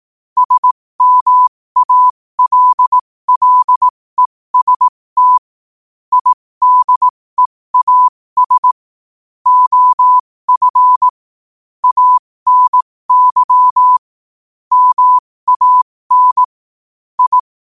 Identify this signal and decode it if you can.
Morse code, heard as a keyed tone, about 18 wpm, spaced as for 15 wpm, 1000 Hz. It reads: SMALLEST IDEAS OF ANY MAN I
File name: qotdfiles / 15wpm